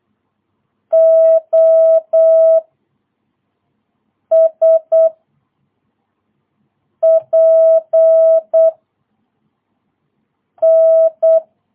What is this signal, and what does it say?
Morse code, heard as a keyed tone, about 8 wpm, 650 Hz. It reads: O S P N
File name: jh8cbh / 2022cw